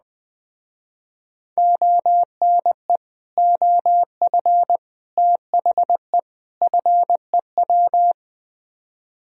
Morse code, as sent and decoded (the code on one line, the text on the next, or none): --- -. / --- ..-. / - .... / ..-. .--
ONE OF THE FEW